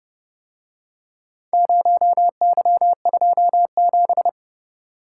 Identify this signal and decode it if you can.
Morse code, heard as a keyed tone, about 30 wpm, 700 Hz. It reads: 0Y27